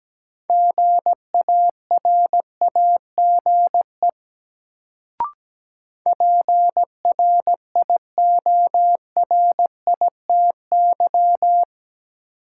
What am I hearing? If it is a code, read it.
Morse code, heard as a keyed tone, about 17 wpm, 700 Hz. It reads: GARAGE PRIORITY
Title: Sentences from Top 1600 Words - New Word Drill - 17wpm